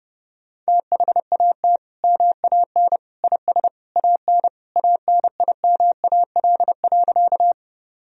Morse code, heard as a keyed tone, about 31 wpm, 700 Hz. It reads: THAT MAN IS AN ANIMAL.
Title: Sentences from Top 1000 Words - Review Part 1 - 30wpm